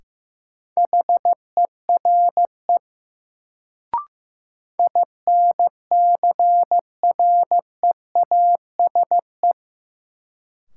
Morse code, heard as a keyed tone, about 15 wpm, 700 Hz. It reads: HERE INCREASE
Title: Sets of 5 Words - New Word Drill - 15wpm